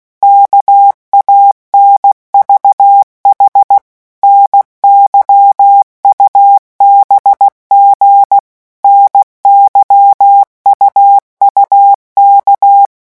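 Morse code, heard as a keyed tone, about 16 wpm, 800 Hz. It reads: KANVH NYUBG NYUUK
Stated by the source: Morse letter-group revision drill at 15 words per minute.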